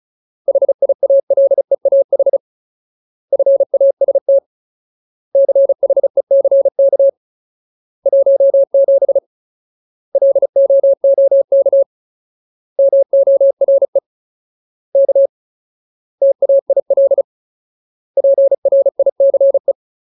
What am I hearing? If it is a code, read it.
Morse code, heard as a keyed tone, about 34 wpm, 550 Hz. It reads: HIALEAH FAST CHECK 17 LOOK MORE K TAIL PRICE
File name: Ultimate Word List / 35WPM Ultimate